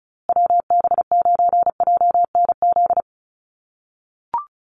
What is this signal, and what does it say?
Morse code, heard as a keyed tone, about 34 wpm, 700 Hz. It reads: WB9JNZ
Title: USA Call Signs Repeated 2x 35wpm